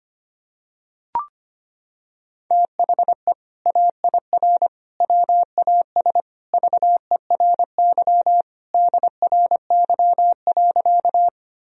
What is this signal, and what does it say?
Morse code, heard as a keyed tone, about 25 wpm, 700 Hz. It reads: THE AIR WAS VERY DRY.
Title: Sentences from Top 1000 Words - Review Part 3 - 25wpm